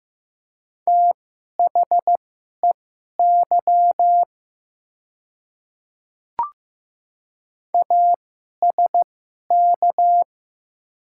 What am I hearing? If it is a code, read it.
Morse code, heard as a keyed tone, about 15 wpm, spaced as for 11 wpm, 700 Hz. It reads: THEY ASK
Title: 2x Character Spacing - Top 200 Words - 15wpm